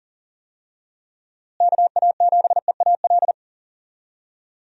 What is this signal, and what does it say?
Morse code, heard as a keyed tone, about 39 wpm, 700 Hz. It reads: KA7EAL
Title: USA Call Signs - 6 Characters - Repeated 3x - 40wpm